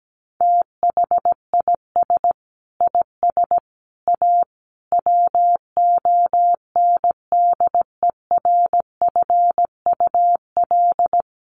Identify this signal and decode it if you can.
Morse code, heard as a keyed tone, about 17 wpm, 700 Hz. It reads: THIS IS A WONDERFUL
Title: Sentences from Top 1100 Words 17wpm